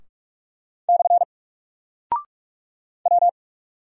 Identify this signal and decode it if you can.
Morse code, heard as a keyed tone, about 44 wpm, 700 Hz. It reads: / W
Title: Single Character (Letter:Number:Punctuation:Prosign) Alt 45wpm